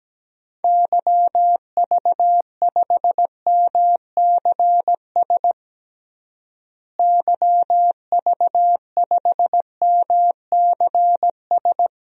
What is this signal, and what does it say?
Morse code, heard as a keyed tone, about 17 wpm, 700 Hz. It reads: YV5MCS YV5MCS
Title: DXCC-100 Call Signs - 6 Characters - Repeated 2x - 17wpm